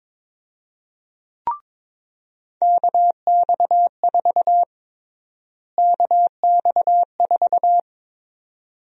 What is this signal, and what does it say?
Morse code, heard as a keyed tone, about 22 wpm, 700 Hz. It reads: KX4 KX4
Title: USA Call Sign Prefixes - Repeated 2x - 22wpm